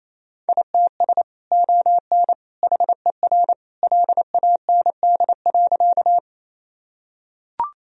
Morse code, heard as a keyed tone, about 28 wpm, 700 Hz. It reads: ITS ON HER LAND.
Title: Sentences from Top 1000 Words - Review Part 1 - 28wpm